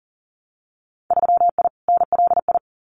Morse code, heard as a keyed tone, about 41 wpm, 700 Hz. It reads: 3I NLI
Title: Winter Field Day Exchanges Encore 40wpm